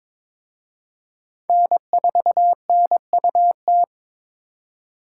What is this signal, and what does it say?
Morse code, heard as a keyed tone, about 26 wpm, 700 Hz. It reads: N4NUT
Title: Speed Racing - USA 1x3 Call Signs 22wpm